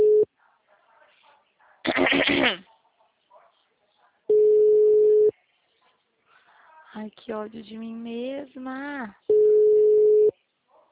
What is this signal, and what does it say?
Ringback tone (Europe, 425 Hz), with speech.